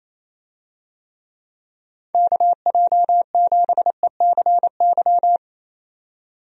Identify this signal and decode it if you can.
Morse code, heard as a keyed tone, about 27 wpm, 700 Hz. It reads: KJ7ECY
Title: USA Call Signs - 6 Characters - 28wpm